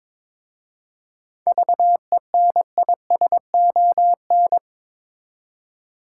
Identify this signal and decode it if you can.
Morse code, heard as a keyed tone, about 22 wpm, 700 Hz. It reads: VENISON
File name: Difficult Words with Hints - 22wpm